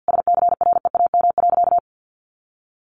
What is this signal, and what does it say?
Morse code, heard as a keyed tone, about 49 wpm, 700 Hz. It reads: SCREAM.
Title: A Sound Of Thunder 50wpm